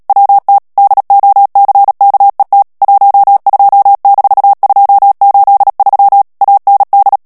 Morse code, heard as a keyed tone, about 37 wpm, 800 Hz. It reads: WT DOCKET 12-283 AND
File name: morse mp3 / ANGLAIS texte clair 3 vitesse 35 mots minute